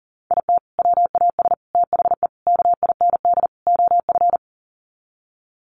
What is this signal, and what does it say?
Morse code, heard as a keyed tone, about 40 wpm, 700 Hz. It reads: IT WAS THE KIND OF
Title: Sets of 5 Words Encore2 40wpm